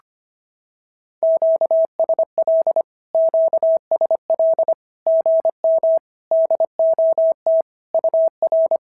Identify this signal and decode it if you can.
Morse code, heard as a keyed tone, about 25 wpm, 650 Hz. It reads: QSL QSL GM DOT UR
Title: SOTA QSO Practice 25wpm